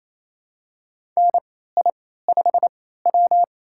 Morse code, heard as a keyed tone, about 28 wpm, spaced as for 16 wpm, 700 Hz. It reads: NI5W
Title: USA Call Signs - 4 Characters - 3x Character Spacing - 28wpm